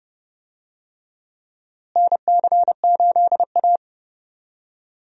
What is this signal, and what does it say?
Morse code, heard as a keyed tone, about 30 wpm, 700 Hz. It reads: NC8A
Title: USA Call Signs - 4 Characters - Repeated 3x - 30wpm